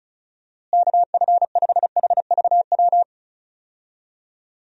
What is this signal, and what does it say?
Morse code, heard as a keyed tone, about 36 wpm, 700 Hz. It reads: KF5HVW
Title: USA Call Signs - 6 Characters - Repeated 3x - 35wpm